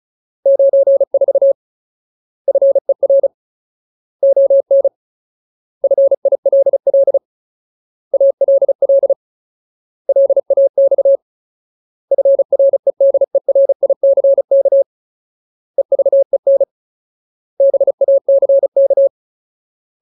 Morse code, heard as a keyed tone, about 35 wpm, 550 Hz. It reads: 94 FER ON FILL ALL LAX FREDERICK EVEN BACK
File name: Ultimate Word List / 35WPM Ultimate